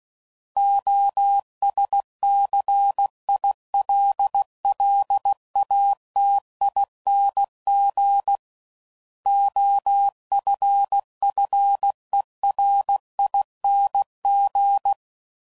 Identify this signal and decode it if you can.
Morse code, heard as a keyed tone, about 16 wpm, 800 Hz. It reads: OSCILLATING OFFERING
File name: Suffixes / Special INGs / Sngl Ings